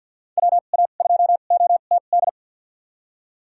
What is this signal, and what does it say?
Morse code, heard as a keyed tone, about 53 wpm, spaced as for 37 wpm, 700 Hz. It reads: WA2OTD
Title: USA Call Signs - 2x Character Spacing - 50wpm